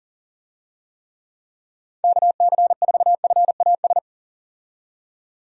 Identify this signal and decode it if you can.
Morse code, heard as a keyed tone, about 40 wpm, 700 Hz. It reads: KC4FAS
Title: USA Call Signs - 6 Characters - 40wpm